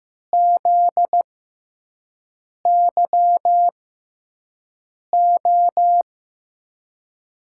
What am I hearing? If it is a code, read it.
Morse code, heard as a keyed tone, about 15 wpm, spaced as for 5 wpm, 700 Hz. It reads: ZYO
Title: USA Call Signs - 6x Character Spacing - 15wpm